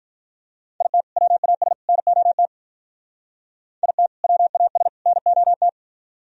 Morse code, heard as a keyed tone, about 46 wpm, 700 Hz. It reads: IT WAS NOT IT WAS NOT